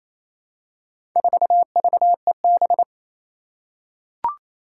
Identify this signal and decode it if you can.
Morse code, heard as a keyed tone, about 28 wpm, 700 Hz. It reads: 4VEB
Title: Four Challenging Letter-Number Combinations 28wpm